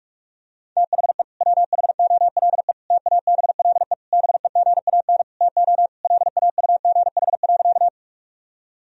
Morse code, heard as a keyed tone, about 46 wpm, 700 Hz. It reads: THE WHOLE TABLE BEGAN TO LAUGH.
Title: Sentences from Top 1000 Words - Review Part 2 - 45wpm